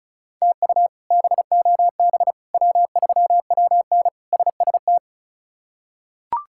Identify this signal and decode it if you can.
Morse code, heard as a keyed tone, about 36 wpm, 700 Hz. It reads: TU BOB W3WN SST
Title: K1USN SST QSO Practice - 35wpm